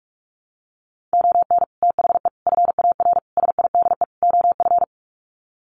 Morse code, heard as a keyed tone, about 44 wpm, 700 Hz. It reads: ON THE FAR SIDE OF
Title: Sets of 5 Words 45wpm